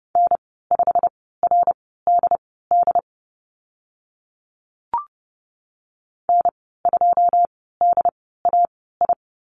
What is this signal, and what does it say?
Morse code, heard as a keyed tone, about 30 wpm, spaced as for 17 wpm, 700 Hz. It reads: N5RDD N2DAI
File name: Lesson 15g - Number 5 - Call Signs 30wpm